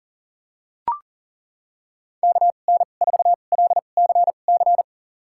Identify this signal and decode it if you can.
Morse code, heard as a keyed tone, about 40 wpm, spaced as for 29 wpm, 700 Hz. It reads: KN4LCC
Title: USA Call Signs - 6 Characters - 2x Character Spacing - 40wpm